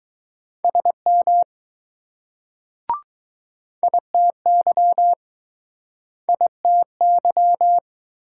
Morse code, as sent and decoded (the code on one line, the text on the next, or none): ... -- / .. - -.-- / .. - -.--
SM ITY ITY